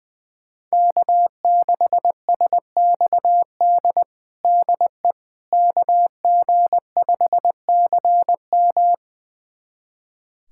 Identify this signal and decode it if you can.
Morse code, heard as a keyed tone, about 21 wpm, 700 Hz. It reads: K6SXD DE KG5CM
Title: USA Callsign DE Callsign - 5 Characters - 20wpm